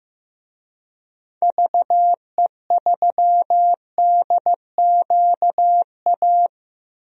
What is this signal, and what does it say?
Morse code, heard as a keyed tone, about 15 wpm, 700 Hz. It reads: VE3DQA
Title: DXCC-100 Call Signs - 6 Characters - Repeated 3x - 15wpm